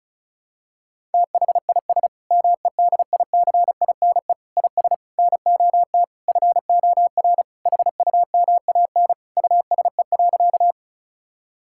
Morse code, heard as a keyed tone, about 36 wpm, 700 Hz. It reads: THIS MEDICINE IS NOT FOR HUMAN USE.
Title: Sentences from Top 1000 Words - Review Part 6 - 35wpm